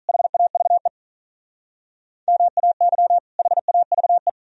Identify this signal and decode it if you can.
Morse code, heard as a keyed tone, about 41 wpm, 690 Hz. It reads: HAVE MAY HAVE
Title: Speed-Racing - Sets of 2 Words 35wpm